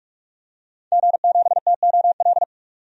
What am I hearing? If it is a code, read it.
Morse code, heard as a keyed tone, about 46 wpm, 700 Hz. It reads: G7TOL
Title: DXCC-100 Call Signs - 5 Characters - 45wpm